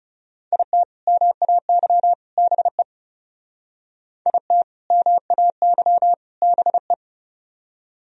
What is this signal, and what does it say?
Morse code, heard as a keyed tone, about 38 wpm, 700 Hz. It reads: IT MAY BE IT MAY BE